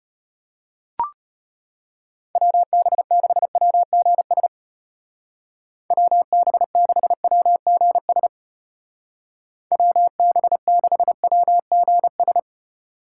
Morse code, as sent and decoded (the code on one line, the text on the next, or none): .-- -... -.... .-- --. ... / .-- -... -.... .-- --. ... / .-- -... -.... .-- --. ...
WB6WGS WB6WGS WB6WGS